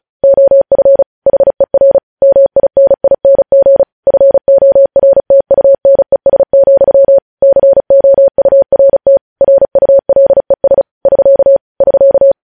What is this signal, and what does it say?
Morse code, heard as a keyed tone, about 36 wpm, 550 Hz. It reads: OF HER MINING FORTUNES, COURT RULES <SK> <SK>